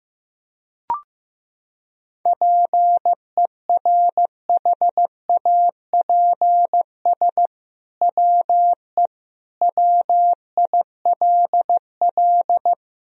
Morse code, heard as a keyed tone, about 16 wpm, 700 Hz. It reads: PERHAPS WE WILL
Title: Sentences from Top 1000 Words - Review Part 2 - 15wpm